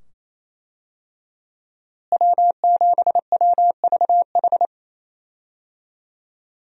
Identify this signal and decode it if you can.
Morse code, heard as a keyed tone, about 27 wpm, 700 Hz. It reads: W7WVH
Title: USA Call Signs - 5 Characters - Repeated 2x - 28wpm